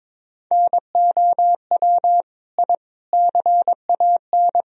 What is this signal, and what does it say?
Morse code, heard as a keyed tone, about 22 wpm, 700 Hz. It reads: NOW I CAN
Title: Sentences from Top 1000 Words - Review Part 1 - 22wpm